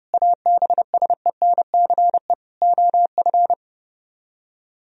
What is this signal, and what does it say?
Morse code, heard as a keyed tone, about 30 wpm, 700 Hz. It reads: ABSENCE OF